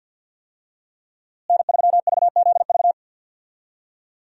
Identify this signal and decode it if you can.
Morse code, heard as a keyed tone, about 46 wpm, 700 Hz. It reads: N3VZV
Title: USA Call Signs - 5 Characters - Repeated 2x - 50wpm